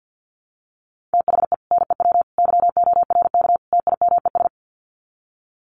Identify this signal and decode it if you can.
Morse code, heard as a keyed tone, about 50 wpm, 700 Hz. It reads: THE NEW YORK TIMES
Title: Sets of 4 Words 50wpm